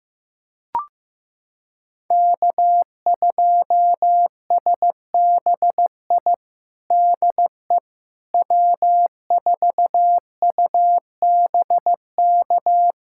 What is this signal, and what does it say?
Morse code, heard as a keyed tone, about 16 wpm, 700 Hz. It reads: K2SBI DE W4UBK
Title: USA Callsign DE Callsign - 5 Characters - 15wpm